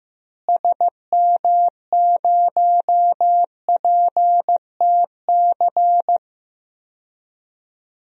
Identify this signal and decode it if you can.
Morse code, heard as a keyed tone, about 15 wpm, 700 Hz. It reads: SM0PTC
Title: DXCC-100 Call Signs - 6 Characters - Repeated 3x - 15wpm